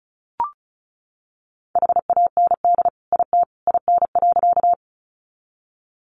Morse code, heard as a keyed tone, about 34 wpm, 700 Hz. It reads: HAND IT IN.
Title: Sentences from Top 200 Words 35wpm